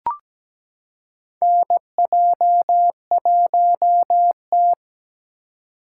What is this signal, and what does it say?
Morse code, heard as a keyed tone, about 17 wpm, 700 Hz. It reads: NJ1T
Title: K1USN SST QSO Practice - 17wpm